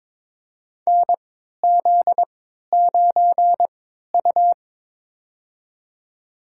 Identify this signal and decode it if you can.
Morse code, heard as a keyed tone, about 22 wpm, spaced as for 12 wpm, 700 Hz. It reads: NZ9U
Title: USA Call Signs - 3x Character Spacing - 22wpm